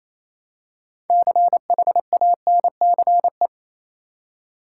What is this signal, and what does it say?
Morse code, heard as a keyed tone, about 28 wpm, 700 Hz. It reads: CHANCE
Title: Six Letter Words - 28wpm